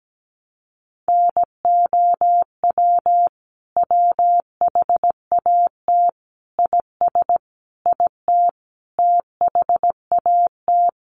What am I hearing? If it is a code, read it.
Morse code, heard as a keyed tone, about 17 wpm, 700 Hz. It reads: NOW WHAT IS IT THAT
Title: Sentences from Top 200 Words 17wpm